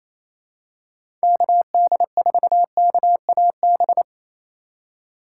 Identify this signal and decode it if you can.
Morse code, heard as a keyed tone, about 28 wpm, 700 Hz. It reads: KD4KAB
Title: USA Call Signs - 6 Characters - 4x Character Spacing - 28wpm